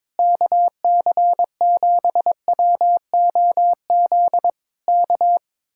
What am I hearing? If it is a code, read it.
Morse code, heard as a keyed tone, about 22 wpm, 700 Hz. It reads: KC7WOZ K
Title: Speed-Racing Common QSO Phrases Encore - 22wpm